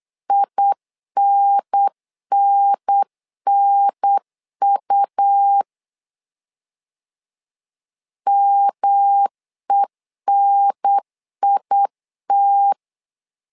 Morse code, heard as a keyed tone, about 8 wpm, 800 Hz. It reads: INNNU MENIT
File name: LEZIONI / 3 MWRO / 3 M 8wpm 2300